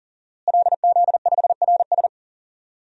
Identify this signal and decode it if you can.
Morse code, heard as a keyed tone, about 40 wpm, 700 Hz. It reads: LZ5RS
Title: DXCC-100 Call Signs - 5 Characters - Repeated 2x - 40wpm